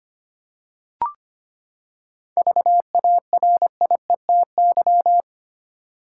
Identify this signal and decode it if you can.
Morse code, heard as a keyed tone, about 25 wpm, 700 Hz. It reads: VARIETY
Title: Seven Letter Words - 25wpm